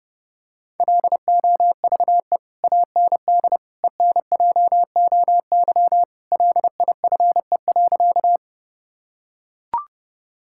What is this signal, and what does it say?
Morse code, heard as a keyed tone, about 32 wpm, 700 Hz. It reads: LOVE AND ENJOY LIFE.
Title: Sentences from Top 1000 Words - Review Part 5 - 30wpm